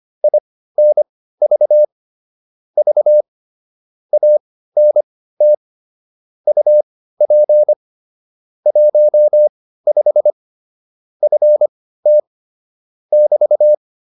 Morse code, heard as a keyed tone, about 25 wpm, spaced as for 15 wpm, 600 Hz. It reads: INV V ANT UP 15 FT <BT>
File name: QSO 302 15